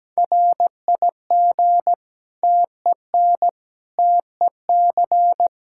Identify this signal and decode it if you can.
Morse code, heard as a keyed tone, about 17 wpm, 700 Hz. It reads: RIG TEN TEC
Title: Practice QSO - Exchange 2 - Info - 17wpm